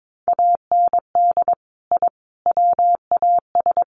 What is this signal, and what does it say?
Morse code, heard as a keyed tone, about 22 wpm, 700 Hz. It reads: AND I WAS
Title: Sets of 3 Words 22wpm